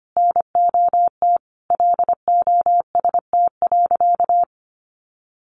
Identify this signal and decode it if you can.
Morse code, heard as a keyed tone, about 25 wpm, 700 Hz. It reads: NOT LOST.
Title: Sentences from Top 1400 Words 25wpm